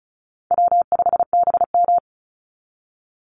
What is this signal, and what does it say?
Morse code, heard as a keyed tone, about 34 wpm, 700 Hz. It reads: W5BM
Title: Easy Call Signs Repeated 3x - 35wpm